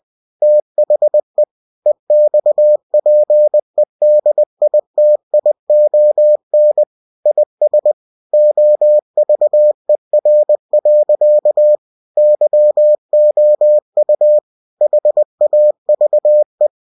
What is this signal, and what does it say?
Morse code, heard as a keyed tone, about 20 wpm, 600 Hz. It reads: THE EXPEDITION IS OVER. YOU HAVE
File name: cw-books / winnie-the-pooh / winnie-20wpm-page-0087